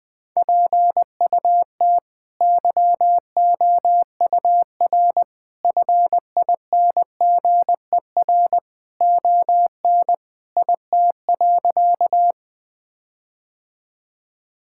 Morse code, heard as a keyed tone, about 20 wpm, 700 Hz. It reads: PUT YOUR FINGER ON IT.
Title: Sentences from Top 1600 Words 20wpm